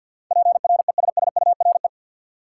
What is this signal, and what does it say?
Morse code, heard as a keyed tone, about 53 wpm, 700 Hz. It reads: PRESSURE
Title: Sentences from Top 1400 Words - New Word Drill Alt - 50wpm